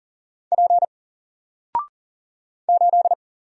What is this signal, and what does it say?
Morse code, heard as a keyed tone, about 40 wpm, 700 Hz. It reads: P 8
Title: Single Character (Letter:Number:Punctuation:Prosign) 40wpm